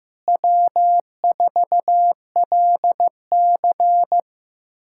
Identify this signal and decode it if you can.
Morse code, heard as a keyed tone, about 15 wpm, 700 Hz. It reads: W4LC
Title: K1USN SST QSO Practice - 15wpm